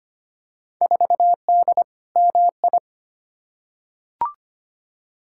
Morse code, heard as a keyed tone, about 25 wpm, 700 Hz. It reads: 4D MI
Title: Field Day Exchanges Encore - 25wpm